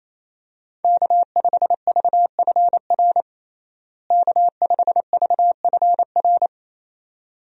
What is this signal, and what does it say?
Morse code, heard as a keyed tone, about 28 wpm, 700 Hz. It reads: K5VFR K5VFR
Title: USA Call Signs - 5 Characters - Repeated 2x - 28wpm